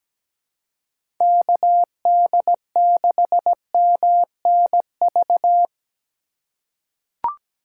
Morse code, heard as a keyed tone, about 17 wpm, 700 Hz. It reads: KD6MNV
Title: USA Call Signs - 6 Characters - 3x Character Spacing - 17wpm